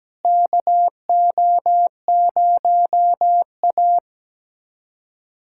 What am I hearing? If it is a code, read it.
Morse code, heard as a keyed tone, about 17 wpm, 700 Hz. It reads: KO0A
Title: USA Callsign DE Callsign - 4 Characters - 17wpm